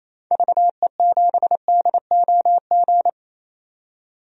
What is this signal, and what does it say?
Morse code, heard as a keyed tone, about 28 wpm, 700 Hz. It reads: VE7DOG
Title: DXCC-100 Call Signs - 6 Characters - 28wpm